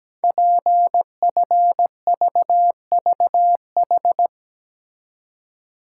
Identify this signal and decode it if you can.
Morse code, heard as a keyed tone, about 17 wpm, 700 Hz. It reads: PFVVH
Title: Code Groups - 5 Letters - 17wpm